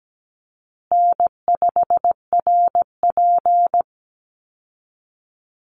Morse code, heard as a keyed tone, about 17 wpm, 700 Hz. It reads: N5RP
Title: Call Signs Repeated 3x - 17wpm